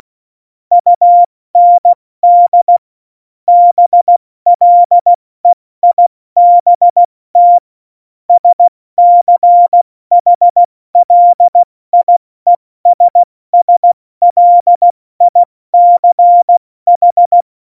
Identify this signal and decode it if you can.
Morse code, heard as a keyed tone, about 16 wpm, 700 Hz. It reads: UND BLEIBT SCHLIESSLICH
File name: Morsekurs / Witze / Witz 20. November 2018, 70 F80